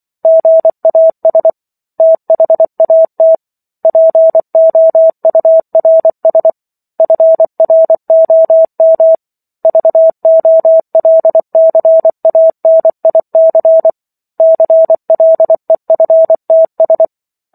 Morse code, heard as a keyed tone, about 24 wpm, 650 Hz. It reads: GAS THAT POURS FROM VOLCANIC CLEFTS